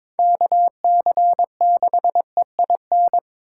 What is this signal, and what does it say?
Morse code, heard as a keyed tone, about 27 wpm, 700 Hz. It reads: KC6EIN
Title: Speed Racing - USA 2x3 Call Signs 22wpm